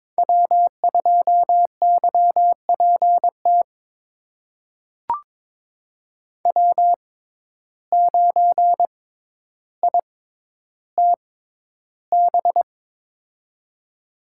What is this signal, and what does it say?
Morse code, heard as a keyed tone, about 22 wpm, spaced as for 11 wpm, 700 Hz. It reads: W2YPT W9ITB
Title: USA Call Signs - 6x Character Spacing - 22wpm